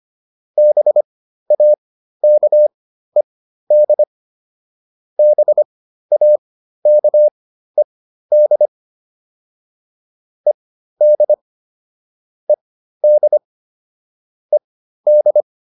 Morse code, heard as a keyed tone, about 25 wpm, spaced as for 13 wpm, 600 Hz. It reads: BAKED BAKED ED ED ED